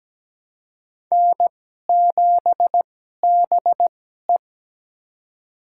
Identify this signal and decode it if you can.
Morse code, heard as a keyed tone, about 17 wpm, spaced as for 12 wpm, 700 Hz. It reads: N7BE
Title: USA Call Signs - 2x Character Spacing - 17wpm